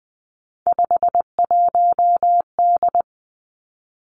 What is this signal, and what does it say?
Morse code, heard as a keyed tone, about 20 wpm, 700 Hz. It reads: 51D
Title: Three Challenging Letter-Number Combinations 20wpm